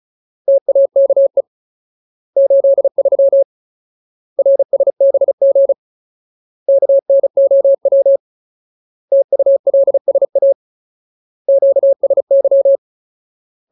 Morse code, heard as a keyed tone, about 36 wpm, 550 Hz. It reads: TAKE 83 RSBG KNOW TULSA QSY
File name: Ultimate Word List / 35WPM Ultimate